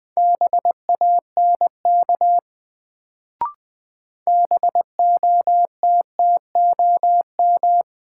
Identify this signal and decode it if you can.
Morse code, heard as a keyed tone, about 20 wpm, 700 Hz. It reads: BANK BOTTOM
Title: Sentences from Top 900 Words - New Word Drill - 20wpm